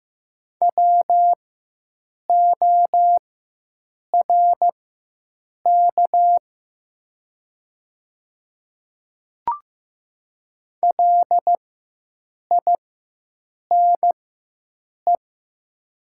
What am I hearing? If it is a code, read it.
Morse code, heard as a keyed tone, about 15 wpm, spaced as for 7 wpm, 700 Hz. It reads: WORK LINE
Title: 4x Character Spacing - Top 200 Words - 15wpm